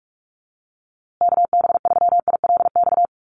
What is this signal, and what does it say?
Morse code, heard as a keyed tone, about 44 wpm, 700 Hz. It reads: KB3ILX
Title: USA Call Signs Repeated 3x 45wpm